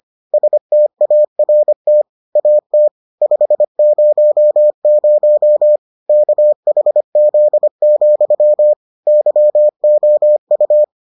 Morse code, heard as a keyed tone, about 25 wpm, 600 Hz. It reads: START AT 500 KHZ, YOU